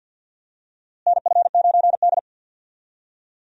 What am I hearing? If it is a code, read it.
Morse code, heard as a keyed tone, about 46 wpm, 700 Hz. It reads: NF9D
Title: USA Call Signs - 4 Characters - Repeated 2x - 50wpm